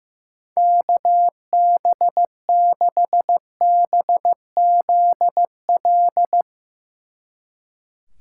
Morse code, heard as a keyed tone, about 15 wpm, 700 Hz. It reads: KB6BZL
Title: Speed Racing - USA Call Signs 15wpm